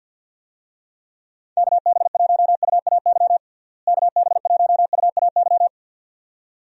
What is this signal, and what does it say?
Morse code, heard as a keyed tone, about 50 wpm, 700 Hz. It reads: KB1UAY KB1UAY
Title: USA Call Signs - 6 Characters - Repeated 2x - 50wpm